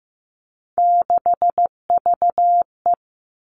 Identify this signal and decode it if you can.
Morse code, heard as a keyed tone, about 15 wpm, 700 Hz. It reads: 6VE
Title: Three Challenging Letter-Number Combinations 15wpm